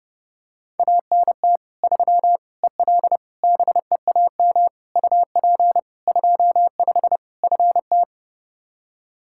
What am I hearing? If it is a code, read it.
Morse code, heard as a keyed tone, about 30 wpm, 700 Hz. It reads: ANT 3 EL BEAM UP 25 FT
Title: K1USN SST QSO Practice 30wpm